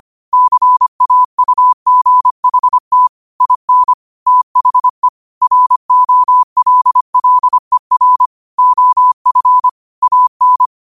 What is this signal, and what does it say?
Morse code, heard as a keyed tone, about 25 wpm, 1000 Hz. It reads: CAUGHT IN THE ROLLER OF AN
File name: qotdfiles / 25wpm